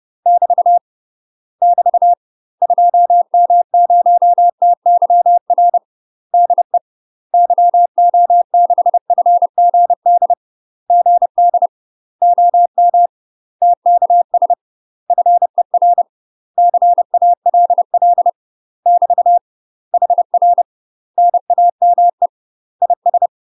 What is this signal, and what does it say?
Morse code, heard as a keyed tone, about 30 wpm, 700 Hz. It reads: <BT> <BT> 2M0TYR DE YO6FGD GD OM TKS FER CALL <BT> HR NAME IS
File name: UK MASTER QSOs TRAINING/30WPM